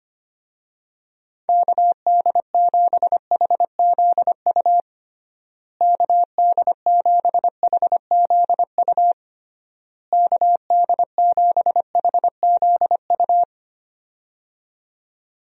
Morse code, heard as a keyed tone, about 25 wpm, 700 Hz. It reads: KD7HZU KD7HZU KD7HZU
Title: USA Call Signs - 6 Characters - Repeated 3x - 25wpm